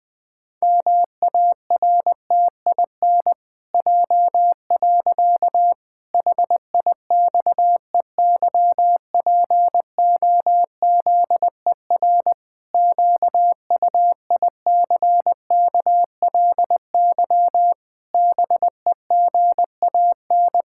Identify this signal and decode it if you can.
Morse code, heard as a keyed tone, about 21 wpm, 700 Hz. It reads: MARTIN J. HIXEYPOZER QUICKLY BEGAN
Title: Pangram Challenge v2 - 20wpm